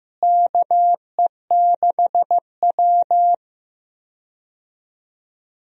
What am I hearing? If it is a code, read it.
Morse code, heard as a keyed tone, about 15 wpm, 700 Hz. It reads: KE6W
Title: USA Call Signs - 4 Characters - Repeated 2x - 15wpm